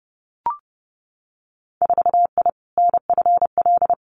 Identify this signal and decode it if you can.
Morse code, heard as a keyed tone, about 30 wpm, 700 Hz. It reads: 4I NFL
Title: Winter Field Day Exchanges Encore 30wpm